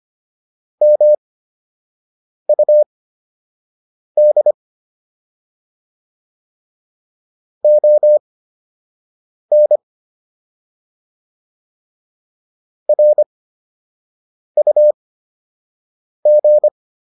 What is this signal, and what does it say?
Morse code, heard as a keyed tone, about 25 wpm, spaced as for 6 wpm, 600 Hz. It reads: MUD ON RUG